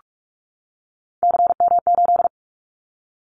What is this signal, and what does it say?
Morse code, heard as a keyed tone, about 46 wpm, 700 Hz. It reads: CM8
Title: Callsign Prefixes for Top 100 DXCC - 45wpm